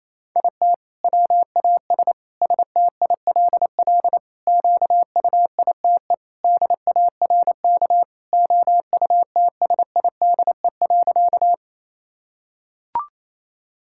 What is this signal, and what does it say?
Morse code, heard as a keyed tone, about 27 wpm, 700 Hz. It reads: IT WAS STILL QUITE DARK OUTSIDE.
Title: Sentences from Top 1000 Words - Review Part 6 - 28wpm